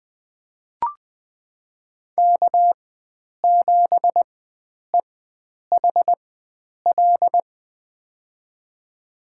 Morse code, heard as a keyed tone, about 20 wpm, spaced as for 9 wpm, 700 Hz. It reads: K7EHL
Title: USA Call Signs - 5 Characters - 4x Character Spacing - 20wpm